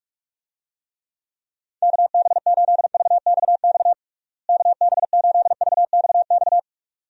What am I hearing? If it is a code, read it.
Morse code, heard as a keyed tone, about 46 wpm, 700 Hz. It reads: KB8VXX KB8VXX
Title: USA Call Signs - 6 Characters - Repeated 2x - 45wpm